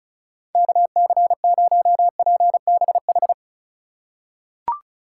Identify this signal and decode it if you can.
Morse code, heard as a keyed tone, about 34 wpm, 700 Hz. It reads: KC0PBH
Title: USA Call Signs - 6 Characters - 3x Character Spacing - 35wpm